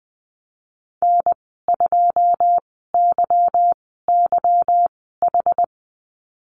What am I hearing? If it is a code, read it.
Morse code, heard as a keyed tone, about 20 wpm, spaced as for 15 wpm, 700 Hz. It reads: N2YYH
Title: 173 Lesson 28g - Number 9 - Call Signs 20wpm